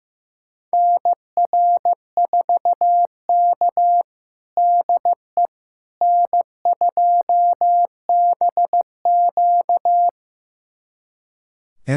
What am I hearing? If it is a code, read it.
Morse code, heard as a keyed tone, about 15 wpm, 700 Hz. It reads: NR4K DE N2BQ
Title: USA Callsign DE Callsign - 4 Characters - 15wpm